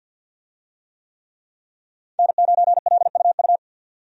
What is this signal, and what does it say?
Morse code, heard as a keyed tone, about 50 wpm, 700 Hz. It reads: N9LUU
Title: USA Call Signs - 5 Characters - 50wpm